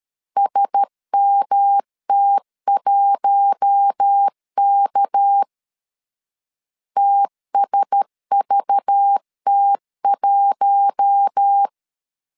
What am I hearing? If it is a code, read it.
Morse code, heard as a keyed tone, about 13 wpm, 800 Hz. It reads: SMT1K TSVT1
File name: CW / DET / 1 6 1500C S B 12wpm 3431